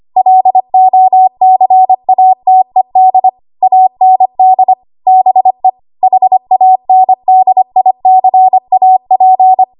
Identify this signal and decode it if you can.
Morse code, heard as a keyed tone, about 25 wpm, 750 Hz. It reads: LOCATED AND BE HANDICAP
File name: Morse / 25WPM